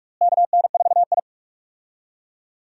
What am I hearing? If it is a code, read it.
Morse code, heard as a keyed tone, about 46 wpm, 700 Hz. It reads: KN4I
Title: USA Call Signs - 4 Characters - Repeated 3x - 45wpm